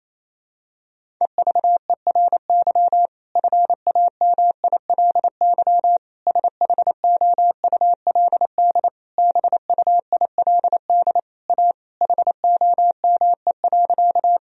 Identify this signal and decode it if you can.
Morse code, heard as a keyed tone, about 28 wpm, 700 Hz. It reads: EVERY FAMILY SHOULD BUILD A HOME.
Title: Sentences from Top 1000 Words - Review Part 1 - 28wpm